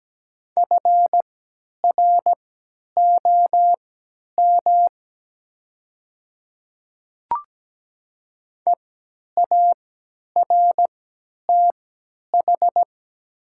Morse code, heard as a keyed tone, about 17 wpm, spaced as for 10 wpm, 700 Hz. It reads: FROM EARTH
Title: 3x Character Spacing - Top 300 Words - 17wpm